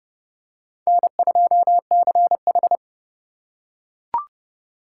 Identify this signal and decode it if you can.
Morse code, heard as a keyed tone, about 29 wpm, 700 Hz. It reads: N2CH